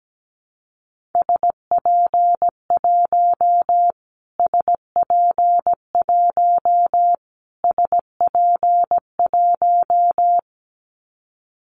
Morse code, heard as a keyed tone, about 17 wpm, 700 Hz. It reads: SP1 SP1 SP1
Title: Callsign Prefixes for Top 25 DXCC - Repeated 3x - 17wpm